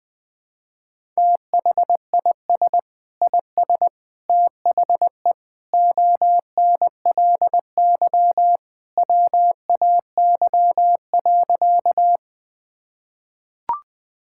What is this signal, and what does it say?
Morse code, heard as a keyed tone, about 21 wpm, 700 Hz. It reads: THIS IS THE ONLY WAY.
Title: Sentences with 2-Word Phrases - 20wpm